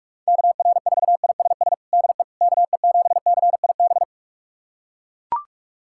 Morse code, heard as a keyed tone, about 41 wpm, 700 Hz. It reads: KR4ISS DE KE7CIB
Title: USA Callsign DE Callsign - 6 Characters - 45wpm